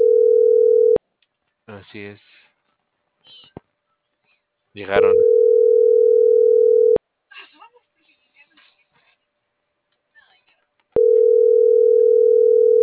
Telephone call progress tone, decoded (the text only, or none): ringback tone